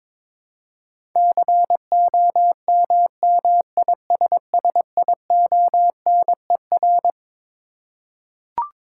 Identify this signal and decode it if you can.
Morse code, heard as a keyed tone, about 22 wpm, 700 Hz. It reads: COMMISSIONER
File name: Twelve Letter Words - 22wpm